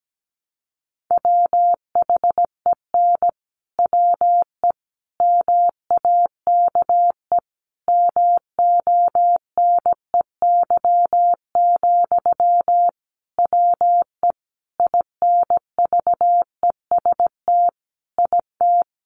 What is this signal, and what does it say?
Morse code, heard as a keyed tone, about 17 wpm, 700 Hz. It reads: WHEN WE MAKE MONEY, WE INVEST IT
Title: Sentences from Top 1100 Words 17wpm